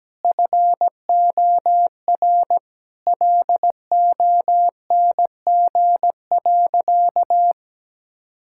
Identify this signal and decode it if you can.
Morse code, heard as a keyed tone, about 17 wpm, 700 Hz. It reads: FOR LONG.
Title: Sentences from Top 1000 Words - Review Part 5 - 17wpm